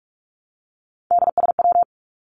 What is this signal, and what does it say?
Morse code, heard as a keyed tone, about 46 wpm, 700 Hz. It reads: DSW
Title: Abbreviations 45wpm